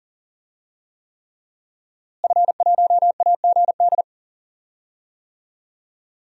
Morse code, heard as a keyed tone, about 40 wpm, 700 Hz. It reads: F1AGD
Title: DXCC-100 Call Signs - 5 Characters - Repeated 3x - 40wpm